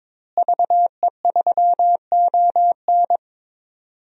Morse code, heard as a keyed tone, about 22 wpm, 700 Hz. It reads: VE3ON
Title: DXCC-100 Call Signs - 5 Characters - Repeated 2x - 22wpm